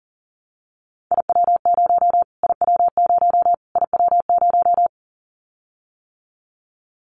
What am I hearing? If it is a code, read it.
Morse code, heard as a keyed tone, about 40 wpm, 700 Hz. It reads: IW0 IW0 IW0
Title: Callsign Prefixes for Top 25 DXCC - Repeated 3x - 40wpm